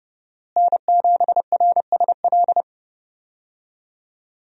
Morse code, heard as a keyed tone, about 30 wpm, 700 Hz. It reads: N7RSL
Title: USA Call Signs - 5 Characters - Repeated 3x - 30wpm